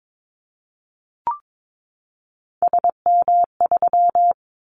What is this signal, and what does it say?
Morse code, heard as a keyed tone, about 22 wpm, 700 Hz. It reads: SM3
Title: Callsign Prefixes for Top 25 DXCC - 22wpm